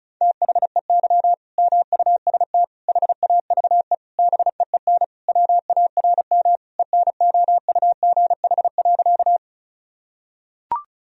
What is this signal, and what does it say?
Morse code, heard as a keyed tone, about 36 wpm, 700 Hz. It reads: THEY MUST HAVE BEEN WARM ENOUGH.
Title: Sentences from Top 1000 Words - Review Part 2 - 35wpm